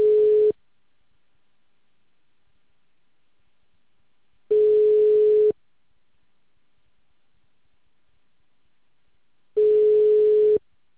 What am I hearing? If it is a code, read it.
Ringback tone (Europe, 425 Hz).